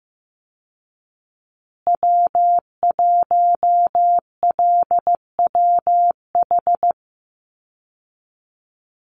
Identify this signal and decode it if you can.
Morse code, heard as a keyed tone, about 14 wpm, 700 Hz. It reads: W1LWH
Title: Call Signs Repeated 3x - 15wpm